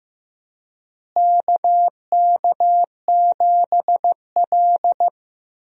Morse code, heard as a keyed tone, about 15 wpm, 700 Hz. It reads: KK7L
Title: USA Call Signs - 4 Characters - Repeated 3x - 15wpm